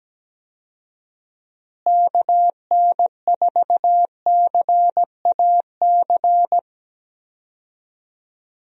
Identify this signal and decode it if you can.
Morse code, heard as a keyed tone, about 17 wpm, 700 Hz. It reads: KN4CAC
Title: USA Call Signs - 6 Characters - Repeated 2x - 17wpm